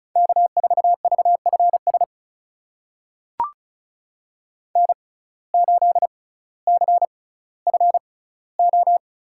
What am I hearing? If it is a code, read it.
Morse code, heard as a keyed tone, about 36 wpm, spaced as for 18 wpm, 700 Hz. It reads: K4VFS N8CFO
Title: USA Call Signs - 5 Characters - 6x Character Spacing - 35wpm